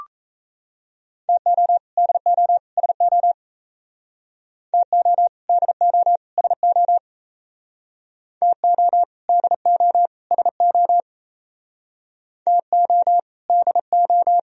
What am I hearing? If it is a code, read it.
Morse code, heard as a keyed tone, about 35 wpm, 700 Hz. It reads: TO DO SO TO DO SO TO DO SO TO DO
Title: Speed-Racing - Sets of 3 Words 28wpm